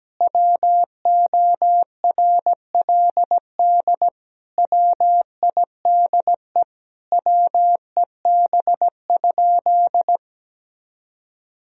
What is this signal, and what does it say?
Morse code, heard as a keyed tone, about 17 wpm, 700 Hz. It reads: WORLD WIDE WEB?
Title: Sentences from Top 1000 Words - Review Part 4 - 17wpm